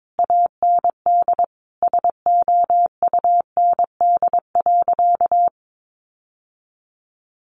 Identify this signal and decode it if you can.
Morse code, heard as a keyed tone, about 22 wpm, 700 Hz. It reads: AND SOUND.
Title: Sentences from Top 900 Words 22wpm